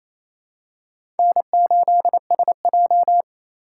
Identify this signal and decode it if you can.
Morse code, heard as a keyed tone, about 28 wpm, 700 Hz. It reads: N8SJ
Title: USA Call Signs - 3x Character Spacing - 28wpm